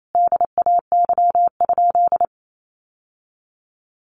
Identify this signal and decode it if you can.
Morse code, heard as a keyed tone, about 28 wpm, 700 Hz. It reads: DAY?
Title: Sentences from Top 1100 Words 28wpm